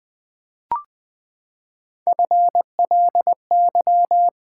Morse code, heard as a keyed tone, about 20 wpm, 700 Hz. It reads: FLY
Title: Three Letter Words - 20wpm